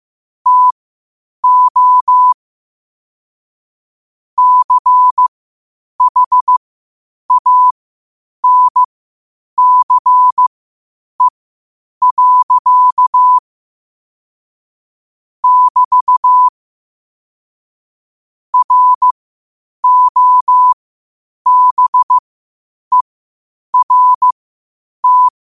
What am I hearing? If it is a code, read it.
Morse code, heard as a keyed tone, about 15 wpm, spaced as for 8 wpm, 1000 Hz. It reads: TO CHANCE. <BT> ROBERT
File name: qotdfiles / 8wpm